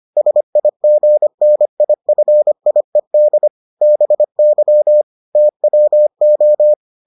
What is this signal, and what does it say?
Morse code, heard as a keyed tone, about 25 wpm, 600 Hz. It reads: SIGNIFIED BY TWO